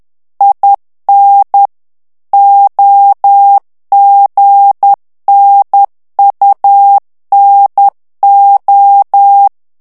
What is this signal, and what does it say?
Morse code, heard as a keyed tone, about 10 wpm, 800 Hz. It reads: IN OGNUNO
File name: ITALIEN texte clair 2 vitesse 10 mots minute